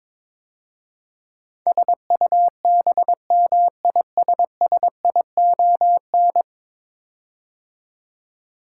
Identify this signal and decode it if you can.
Morse code, heard as a keyed tone, about 22 wpm, 700 Hz. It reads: SUBMISSION
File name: practice / 22wpm / 10 Letter Words - First 3 Letters Given - 22wpm